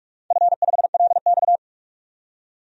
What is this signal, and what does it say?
Morse code, heard as a keyed tone, about 44 wpm, 700 Hz. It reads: F5LX
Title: DXCC-100 Call Signs - 4 Characters - 45wpm